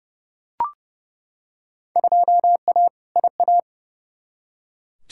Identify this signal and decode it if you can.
Morse code, heard as a keyed tone, about 30 wpm, 700 Hz. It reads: 2A IA
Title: Field Day Exchanges - 30wpm